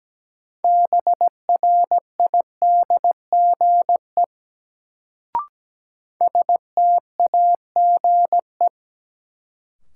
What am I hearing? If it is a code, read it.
Morse code, heard as a keyed tone, about 17 wpm, 700 Hz. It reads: BRIDGE STAGE
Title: Sentences from Top 1200 Words - New Word Drill - 17wpm